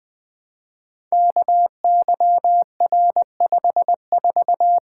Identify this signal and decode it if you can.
Morse code, heard as a keyed tone, about 20 wpm, 700 Hz. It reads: KYR54
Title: Code Groups - 5 Letter:Numbers - 20wpm